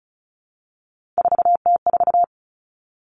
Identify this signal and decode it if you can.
Morse code, heard as a keyed tone, about 36 wpm, 700 Hz. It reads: 4T4
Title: Three Challenging Letter-Number Combinations 35wpm